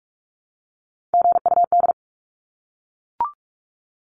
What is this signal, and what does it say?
Morse code, heard as a keyed tone, about 44 wpm, 700 Hz. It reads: GUD
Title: Abbreviations 45wpm